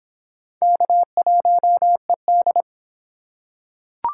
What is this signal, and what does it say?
Morse code, heard as a keyed tone, about 23 wpm, 700 Hz. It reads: K1ED